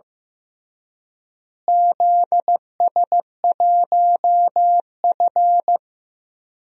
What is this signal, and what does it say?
Morse code, heard as a keyed tone, about 15 wpm, 700 Hz. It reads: ZS1F
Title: DXCC-100 Call Signs - 4 Characters - Repeated 3x - 15wpm